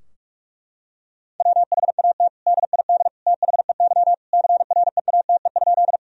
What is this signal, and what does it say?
Morse code, heard as a keyed tone, about 46 wpm, 700 Hz. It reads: WHAT DID THEY CREATE?
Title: Sentences from Top 1000 Words - Review Part 3 - 45wpm